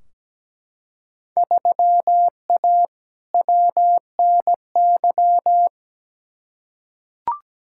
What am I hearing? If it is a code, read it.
Morse code, heard as a keyed tone, about 17 wpm, 700 Hz. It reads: 3A WNY
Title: Field Day Exchanges - 17wpm